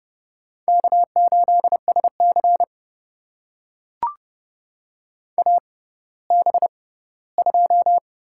Morse code, heard as a keyed tone, about 30 wpm, spaced as for 17 wpm, 700 Hz. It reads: K8SC AB2
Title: USA Call Signs - 6x Character Spacing - 30wpm